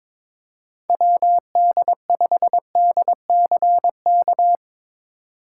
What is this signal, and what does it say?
Morse code, heard as a keyed tone, about 23 wpm, 700 Hz. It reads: WD5DCK
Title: USA Call Signs - 6 Characters - 2x Character Spacing - 22wpm